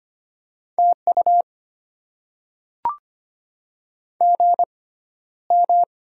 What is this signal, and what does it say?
Morse code, heard as a keyed tone, about 25 wpm, spaced as for 13 wpm, 700 Hz. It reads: TU GM
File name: Top 20 QSO Elements - 6x Character Spacing - 25wpm